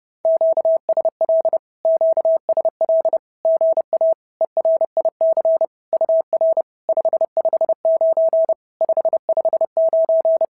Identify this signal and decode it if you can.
Morse code, heard as a keyed tone, about 30 wpm, 650 Hz. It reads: QSL QSL GA ERIC UR 559 559
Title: SOTA QSO Practice 30wpm